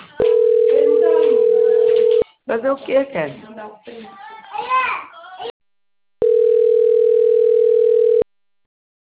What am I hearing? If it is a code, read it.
Ringback tone (North America), 440 + 480 Hz, with speech.
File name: rg-1600-unknown- - - .35213